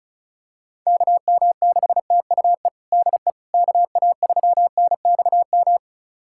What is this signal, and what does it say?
Morse code, heard as a keyed tone, about 36 wpm, 700 Hz. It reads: KM6TUE DE KA3NXM
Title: USA Callsign DE Callsign 35wpm